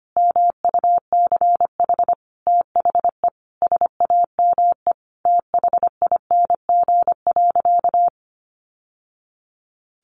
Morse code, heard as a keyed tone, about 25 wpm, 700 Hz. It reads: MUCH THE SAME THING.
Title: Sentences from Top 1100 Words 25wpm